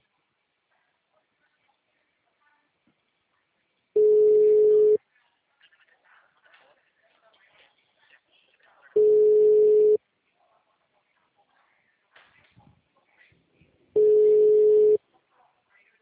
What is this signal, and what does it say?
Ringback tone (Europe, 425 Hz).